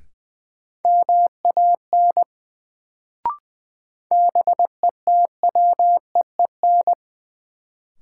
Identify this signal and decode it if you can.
Morse code, heard as a keyed tone, about 20 wpm, 700 Hz. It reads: MAN BETWEEN
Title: Sets of 2 Words - New Word Drill - 20wpm